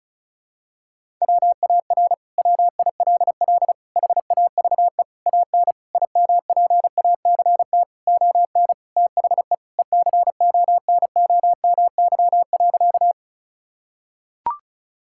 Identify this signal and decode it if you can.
Morse code, heard as a keyed tone, about 34 wpm, 700 Hz. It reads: WAR WILL HAVE AN IMPACT ON THE ECONOMY.
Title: Sentences from Top 1000 Words - Review Part 6 - 35wpm